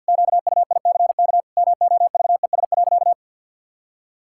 Morse code, heard as a keyed tone, about 50 wpm, 700 Hz. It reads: QUICK MOVES.
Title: Pangram Challenge v2 - 50wpm